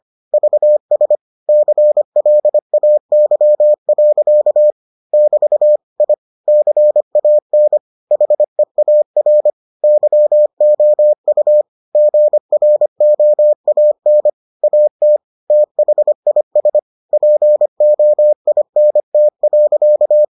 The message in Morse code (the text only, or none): VS CLAY. <BT> I CAN HEAR YOU GROAN AT THIS POINT.